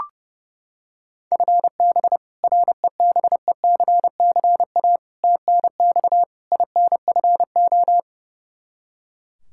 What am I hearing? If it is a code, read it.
Morse code, heard as a keyed tone, about 29 wpm, 700 Hz. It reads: FB REBECCA TNX INFO
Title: Practice QSO - Exchange 3 - More Info - 30wpm